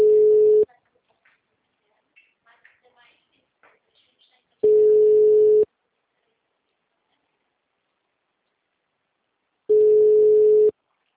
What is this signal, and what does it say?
Ringback tone (Europe, 425 Hz), with speech.
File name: out- -2063- - - .81957